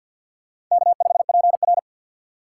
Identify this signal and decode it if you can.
Morse code, heard as a keyed tone, about 48 wpm, 700 Hz. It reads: K5PR
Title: USA Call Signs - 4x Character Spacing - 50wpm